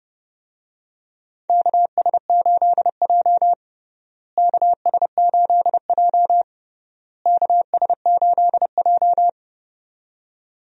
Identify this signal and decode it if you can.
Morse code, heard as a keyed tone, about 30 wpm, 700 Hz. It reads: KS8J KS8J KS8J
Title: USA Call Signs - 4 Characters - Repeated 3x - 30wpm